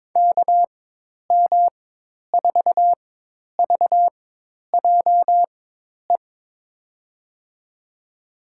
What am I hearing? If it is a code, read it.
Morse code, heard as a keyed tone, about 21 wpm, spaced as for 10 wpm, 700 Hz. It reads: KM4VJE
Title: USA Call Signs - 4x Character Spacing - 22wpm